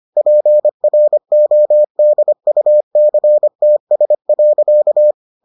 Morse code, heard as a keyed tone, about 25 wpm, 600 Hz. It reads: PRODUCTS.